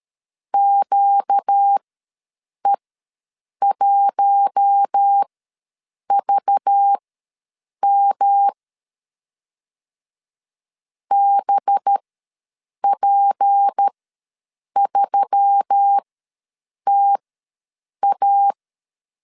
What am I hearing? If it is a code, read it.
Morse code, heard as a keyed tone, about 13 wpm, spaced as for 7 wpm, 800 Hz. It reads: QE1VM BP3TA
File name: LEZIONI / 6 PKX0 / 6 P ws 12wpm 3117